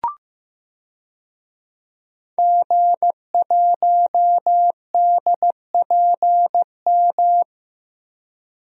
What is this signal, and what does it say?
Morse code, heard as a keyed tone, about 16 wpm, 700 Hz. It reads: G1DPM
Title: DXCC-100 Call Signs - 5 Characters - Repeated 3x - 15wpm